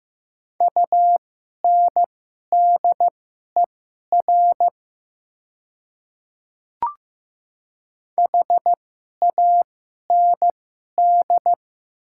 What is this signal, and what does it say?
Morse code, heard as a keyed tone, about 15 wpm, spaced as for 11 wpm, 700 Hz. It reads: UNDER HAND
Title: 2x Character Spacing - Top 200 Words - 15wpm